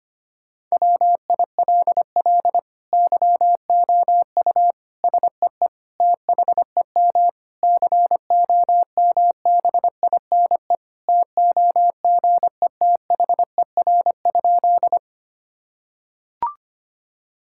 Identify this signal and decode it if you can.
Morse code, heard as a keyed tone, about 25 wpm, 700 Hz. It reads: WILL YOU SEE THEM COMBINE TOGETHER?
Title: Sentences from Top 1600 Words 25wpm